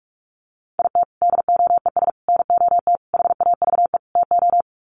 Morse code, heard as a keyed tone, about 44 wpm, 700 Hz. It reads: IT DOES NOT HAVE TO
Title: Sets of 5 Words 45wpm